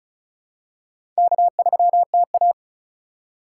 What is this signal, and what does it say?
Morse code, heard as a keyed tone, about 36 wpm, 690 Hz. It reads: K3TA
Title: USA Call Signs - 4x Character Spacing - 35wpm